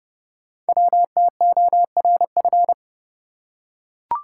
Morse code, heard as a keyed tone, about 30 wpm, 700 Hz. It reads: WTORF
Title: Code Groups - 5 Letter:Numbers - 30wpm